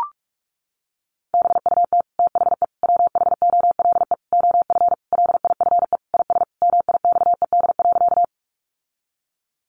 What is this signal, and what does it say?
Morse code, heard as a keyed tone, about 44 wpm, 700 Hz. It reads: BUT THE WHOLE OF LIFE IS MIXED.
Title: Sentences from Top 1500 Words 45wpm